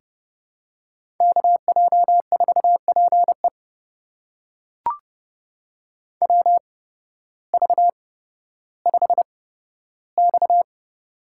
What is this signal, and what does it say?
Morse code, heard as a keyed tone, about 30 wpm, spaced as for 14 wpm, 700 Hz. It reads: KJ4PE WV5X
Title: USA Call Signs - 8x Character Spacing - 30wpm